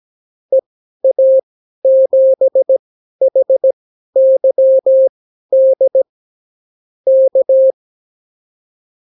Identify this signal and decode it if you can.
Morse code, heard as a keyed tone, about 17 wpm, spaced as for 12 wpm, 530 Hz. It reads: EA7HYD K
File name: CW QSO libroCW4 18-8ppm